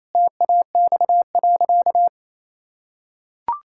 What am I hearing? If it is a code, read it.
Morse code, heard as a keyed tone, about 28 wpm, 700 Hz. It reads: TAX.
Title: Sentences from Top 1000 Words - Review Part 5 - 28wpm